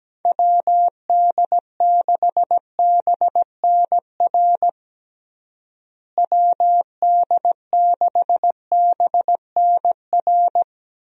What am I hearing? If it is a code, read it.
Morse code, heard as a keyed tone, about 17 wpm, 700 Hz. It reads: WD6BNR WD6BNR
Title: USA Call Signs - 6 Characters - Repeated 3x - 17wpm